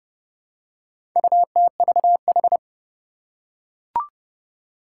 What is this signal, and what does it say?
Morse code, heard as a keyed tone, about 30 wpm, 700 Hz. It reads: UTVH
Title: Four Challenging Letter-Number Combinations 30wpm